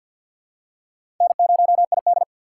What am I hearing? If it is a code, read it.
Morse code, heard as a keyed tone, about 50 wpm, 700 Hz. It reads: N0ID
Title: USA Call Signs - 3x Character Spacing - 50wpm